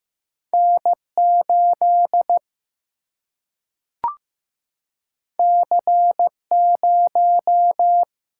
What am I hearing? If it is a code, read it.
Morse code, heard as a keyed tone, about 15 wpm, 700 Hz. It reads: N 8 C 0
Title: Single-Letter Single-Number 15wpm